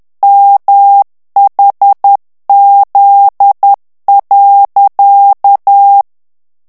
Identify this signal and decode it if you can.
Morse code, heard as a keyed tone, about 10 wpm, 800 Hz. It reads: MHZ.
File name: ANGLAIS texte clair 3 vitesse 10 mots minute